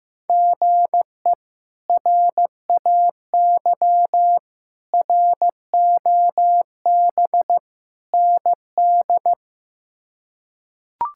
Morse code, heard as a keyed tone, about 15 wpm, 700 Hz. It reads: GE RAY ROB ND
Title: K1USN SST QSO Practice - 15wpm